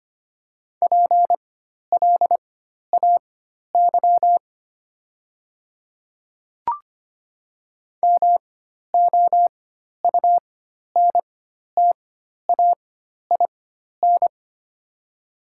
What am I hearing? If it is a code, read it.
Morse code, heard as a keyed tone, about 25 wpm, spaced as for 12 wpm, 700 Hz. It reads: PLAY MOUNTAIN
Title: 4x Character Spacing - Top 300 Words - 25wpm